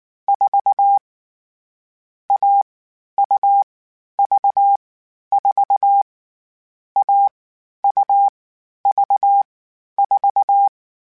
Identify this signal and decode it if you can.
Morse code, heard as a keyed tone, about 19 wpm, 800 Hz. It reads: 4 A U V 4 A U V 4